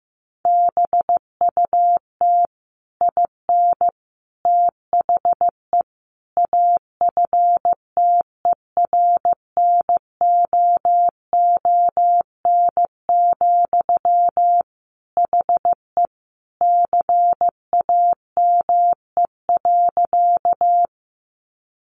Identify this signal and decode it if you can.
Morse code, heard as a keyed tone, about 14 wpm, 700 Hz. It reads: BUT IN THE AFTERNOON, HE CAME.
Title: Sentences from Top 1500 Words 15wpm